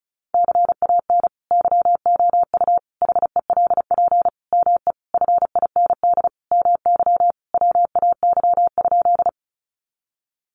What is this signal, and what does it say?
Morse code, heard as a keyed tone, about 34 wpm, 700 Hz. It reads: CAN YOU HELP ME FIND MY WAY?
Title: Sentences from Top 200 Words 35wpm